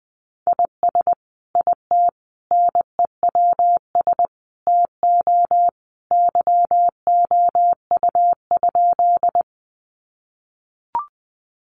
Morse code, heard as a keyed tone, about 20 wpm, 700 Hz. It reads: IS IT NEWS TO YOU?
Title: Sentences from Top 800 Words 20wpm